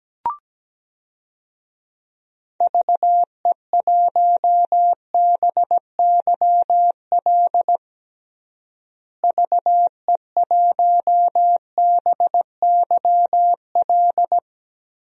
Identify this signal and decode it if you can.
Morse code, heard as a keyed tone, about 17 wpm, 700 Hz. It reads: VE1BYL VE1BYL
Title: DXCC-100 Call Signs - 6 Characters - Repeated 3x - 17wpm